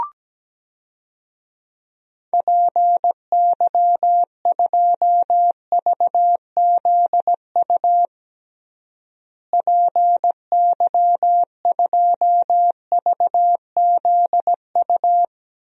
Morse code, heard as a keyed tone, about 17 wpm, 700 Hz. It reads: PY2VZU PY2VZU
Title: DXCC-100 Call Signs - 6 Characters - Repeated 3x - 17wpm